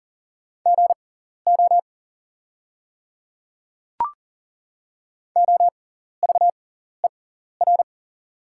Morse code, heard as a keyed tone, about 40 wpm, spaced as for 14 wpm, 700 Hz. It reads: GO OVER
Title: 6x Character Spacing - Top 100 Words - 40wpm